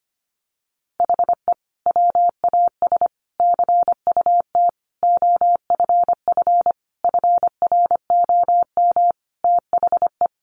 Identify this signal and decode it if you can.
Morse code, heard as a keyed tone, about 25 wpm, 700 Hz. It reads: HE WAS CUT OFF FROM THE
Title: Sentences from Top 1100 Words 25wpm